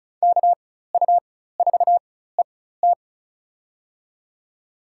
Morse code, heard as a keyed tone, about 36 wpm, spaced as for 16 wpm, 700 Hz. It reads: KU4ET
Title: USA Call Signs - 4x Character Spacing - 35wpm